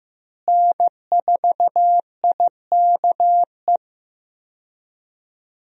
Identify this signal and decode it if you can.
Morse code, heard as a keyed tone, about 15 wpm, 700 Hz. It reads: N4IKE
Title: USA Call Signs - 5 Characters - Repeated 3x - 15wpm